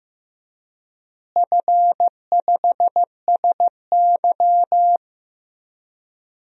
Morse code, heard as a keyed tone, about 15 wpm, 700 Hz. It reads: F5SY
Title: DXCC-100 Call Signs - 4 Characters - Repeated 3x - 15wpm